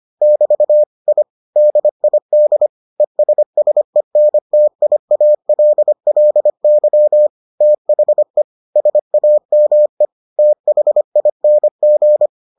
Morse code, heard as a keyed tone, about 25 wpm, 600 Hz. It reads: <BT> I DID ESSENTIALLY THE SAME THING